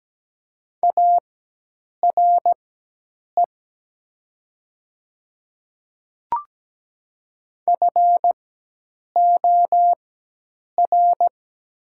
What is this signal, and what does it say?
Morse code, heard as a keyed tone, about 17 wpm, spaced as for 8 wpm, 700 Hz. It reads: ARE FOR
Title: 4x Character Spacing - Top 100 Words - 17wpm